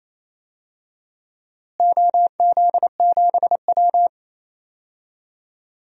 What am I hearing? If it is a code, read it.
Morse code, heard as a keyed tone, about 28 wpm, 700 Hz. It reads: OZ7W
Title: DXCC-100 Call Signs - 4 Characters - Repeated 2x - 28wpm